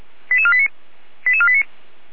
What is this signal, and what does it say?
Five-tone selective call (ZVEI), heard as quick runs of five tones, tones of about 70 ms. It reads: 88489 88489